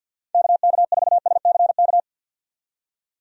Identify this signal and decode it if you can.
Morse code, heard as a keyed tone, about 51 wpm, 700 Hz. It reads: KK4SCK
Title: USA Call Signs - 6 Characters - 50wpm